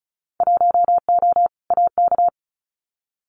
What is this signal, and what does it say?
Morse code, heard as a keyed tone, about 36 wpm, 700 Hz. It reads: 1O AK
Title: Winter Field Day Exchanges Encore 35wpm